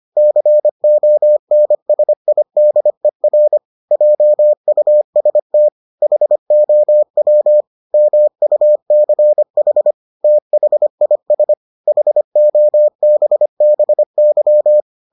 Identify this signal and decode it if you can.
Morse code, heard as a keyed tone, about 25 wpm, 600 Hz. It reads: CONSIDER JUST HOW MUCH THIS HOBBY